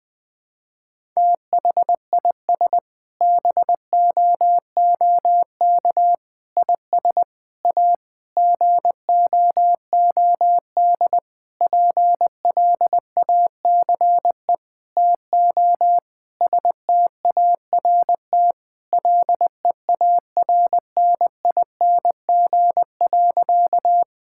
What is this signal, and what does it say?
Morse code, heard as a keyed tone, about 20 wpm, 700 Hz. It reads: THIS BOOK IS A GOOD PLACE TO START LEARNING.
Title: Sentences with 6-Word Phrases - 20wpm